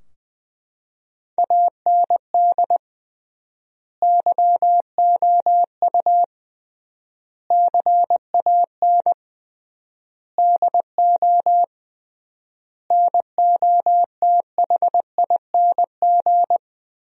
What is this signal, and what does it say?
Morse code, heard as a keyed tone, about 20 wpm, 700 Hz. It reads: AND YOU CAN DO NOTHING
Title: 3x Word-Spacing - Sentences from Top 500 Words - 20wpm